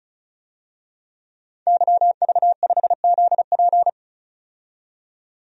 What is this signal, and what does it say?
Morse code, heard as a keyed tone, about 36 wpm, 700 Hz. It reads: YV5ZP
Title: DXCC-100 Call Signs - 5 Characters - 35wpm